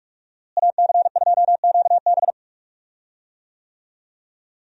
Morse code, heard as a keyed tone, about 42 wpm, 700 Hz. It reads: AC2QB
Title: USA Call Signs - 5 Characters - Repeated 2x - 45wpm